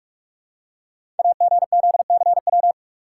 Morse code, heard as a keyed tone, about 44 wpm, 700 Hz. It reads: AGZCW
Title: Code Groups - 5 Letters - 45wpm